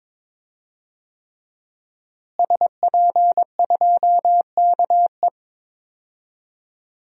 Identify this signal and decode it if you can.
Morse code, heard as a keyed tone, about 22 wpm, 700 Hz. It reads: SP2KE
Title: DXCC-100 Call Signs - 5 Characters - Repeated 3x - 22wpm